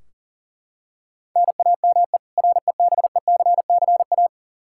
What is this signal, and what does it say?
Morse code, heard as a keyed tone, about 40 wpm, 700 Hz. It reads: NAME REBECCA
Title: K1USN SST QSO Practice 40wpm